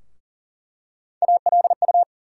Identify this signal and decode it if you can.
Morse code, heard as a keyed tone, about 40 wpm, 700 Hz. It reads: ALU
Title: USA Call Sign Postfixes - 3 Letters 40wpm